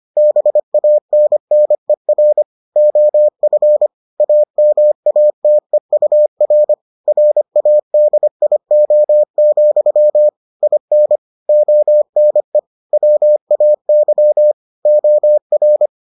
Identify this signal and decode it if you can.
Morse code, heard as a keyed tone, about 25 wpm, 600 Hz. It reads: BANNER OF AMATEUR RADIO, IN ONE WAY OR